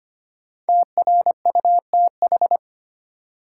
Morse code, heard as a keyed tone, about 25 wpm, 700 Hz. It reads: TRUTH
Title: Sentences from Top 800 Words - New Word Drill - 25wpm